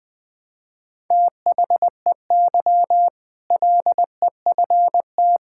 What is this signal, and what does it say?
Morse code, heard as a keyed tone, about 21 wpm, 700 Hz. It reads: THEY LEFT
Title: Sentences with 2-Word Phrases - 20wpm